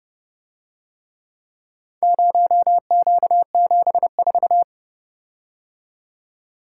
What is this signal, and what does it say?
Morse code, heard as a keyed tone, about 30 wpm, 700 Hz. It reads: 0Q74